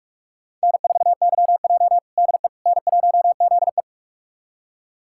Morse code, heard as a keyed tone, about 46 wpm, 700 Hz. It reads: N4YJ DE N1ZE
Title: USA Callsign DE Callsign - 4 Characters - 45wpm